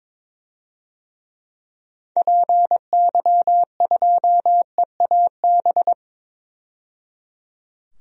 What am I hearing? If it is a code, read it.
Morse code, heard as a keyed tone, about 22 wpm, 700 Hz. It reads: PY2EAB
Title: DXCC-100 Call Signs - 6 Characters - 22wpm